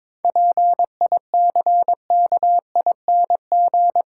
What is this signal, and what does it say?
Morse code, heard as a keyed tone, about 22 wpm, 700 Hz. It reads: PICKING
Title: Pangram Challenge v2 - 22wpm